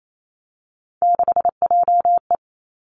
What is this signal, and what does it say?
Morse code, heard as a keyed tone, about 28 wpm, 700 Hz. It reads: 6JE
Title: Three Challenging Letter-Number Combinations 28wpm